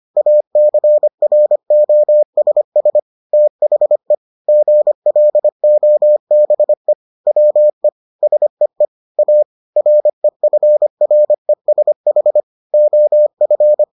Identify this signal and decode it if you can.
Morse code, heard as a keyed tone, about 25 wpm, 600 Hz. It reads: ACROSS THE GLOBE WE SEE A REFRESH OF